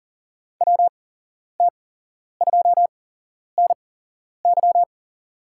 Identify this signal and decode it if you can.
Morse code, heard as a keyed tone, about 40 wpm, spaced as for 11 wpm, 700 Hz. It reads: WT2NY
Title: USA Call Signs - 5 Characters - 8x Character Spacing - 40wpm